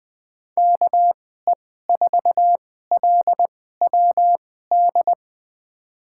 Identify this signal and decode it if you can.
Morse code, heard as a keyed tone, about 20 wpm, spaced as for 14 wpm, 700 Hz. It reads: KE4LWD
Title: USA Call Signs - 6 Characters - 2x Character Spacing - 20wpm